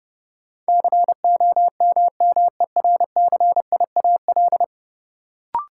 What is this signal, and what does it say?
Morse code, heard as a keyed tone, about 30 wpm, 700 Hz. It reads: COMMERCIAL
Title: Sentences from Top 1000 Words - Review Part 5 - New Word Drill - 30wpm